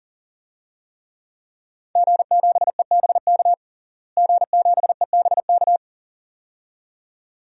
Morse code, heard as a keyed tone, about 40 wpm, 700 Hz. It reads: G7EBK G7EBK
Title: DXCC-100 Call Signs - 5 Characters - Repeated 2x - 40wpm